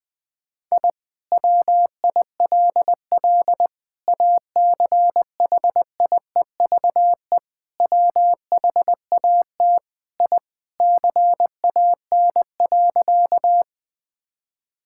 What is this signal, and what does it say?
Morse code, heard as a keyed tone, about 21 wpm, 700 Hz. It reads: I WILL ACHIEVE WHAT I CAN.
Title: Sentences from Top 1000 Words - Review Part 6 - 20wpm